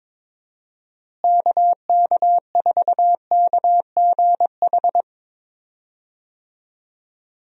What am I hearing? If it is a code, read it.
Morse code, heard as a keyed tone, about 21 wpm, 700 Hz. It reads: KK4KGH
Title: USA Call Signs - 6 Characters - Repeated 2x - 22wpm